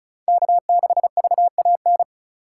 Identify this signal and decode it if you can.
Morse code, heard as a keyed tone, about 36 wpm, 700 Hz. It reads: K6VAN
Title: USA Call Signs - 5 Characters - 4x Character Spacing - 35wpm